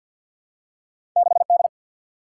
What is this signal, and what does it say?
Morse code, heard as a keyed tone, about 53 wpm, 700 Hz. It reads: 6 D
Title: Single-Letter Single-Number 50wpm